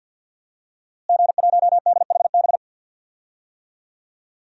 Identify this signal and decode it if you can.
Morse code, heard as a keyed tone, about 50 wpm, 700 Hz. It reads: G1DHB
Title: DXCC-100 Call Signs - 5 Characters - Repeated 3x - 50wpm